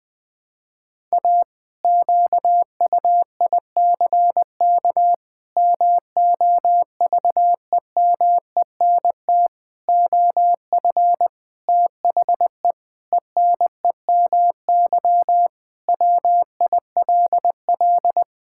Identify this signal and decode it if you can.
Morse code, heard as a keyed tone, about 20 wpm, 700 Hz. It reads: A QUICK MOVEMENT OF THE ENEMY WILL
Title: Pangram Challenge v2 - 20wpm